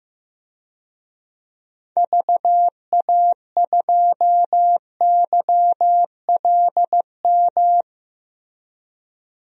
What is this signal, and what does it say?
Morse code, heard as a keyed tone, about 15 wpm, 700 Hz. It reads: VA2YLM
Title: DXCC-100 Call Signs - 6 Characters - Repeated 2x - 15wpm